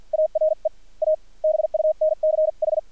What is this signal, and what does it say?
Morse code, heard as a keyed tone, about 50 wpm, spaced as for 43 wpm, 620 Hz. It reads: ARE A BUNCH
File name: archived / CRSnet